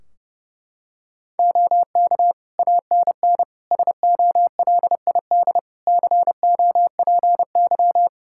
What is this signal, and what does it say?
Morse code, heard as a keyed tone, about 30 wpm, 700 Hz. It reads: OK ANN SOLID COPY
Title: K1USN SST QSO Practice 30wpm